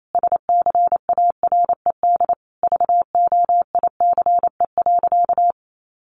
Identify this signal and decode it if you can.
Morse code, heard as a keyed tone, about 28 wpm, 700 Hz. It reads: SCARED VOICE.
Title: Sentences from Top 700 Words 28wpm